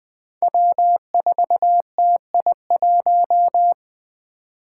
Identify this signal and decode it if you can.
Morse code, heard as a keyed tone, about 20 wpm, 700 Hz. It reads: W4TI1
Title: Five Challenging Letter-Number Combinations 20wpm